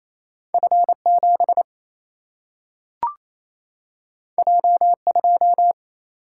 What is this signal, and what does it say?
Morse code, heard as a keyed tone, about 28 wpm, 700 Hz. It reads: F 7 J 2
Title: Single-Letter Single-Number 28wpm